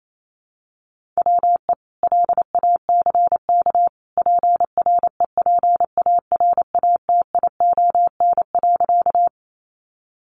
Morse code, heard as a keyed tone, about 28 wpm, 700 Hz. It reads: WE LACK PREPARATION.
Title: Sentences from Top 1400 Words 28wpm